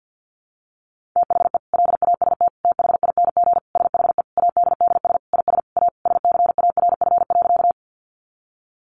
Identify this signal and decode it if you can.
Morse code, heard as a keyed tone, about 50 wpm, 700 Hz. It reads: THE LAST THING SHE ADDS IS A SCARF.
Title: Easy Phrases 50wpm